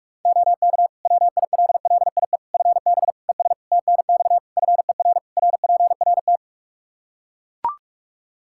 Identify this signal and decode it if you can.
Morse code, heard as a keyed tone, about 41 wpm, 700 Hz. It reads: OK WILLIE FB ES TNX FER RPRT
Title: Practice QSO - Exchange 2 - Info - 45wpm